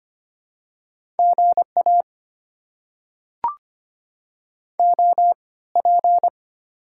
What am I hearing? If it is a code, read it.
Morse code, heard as a keyed tone, about 25 wpm, spaced as for 18 wpm, 690 Hz. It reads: GA OP
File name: Top 90 QSO Elements - 3x Character Spacing - 25wpm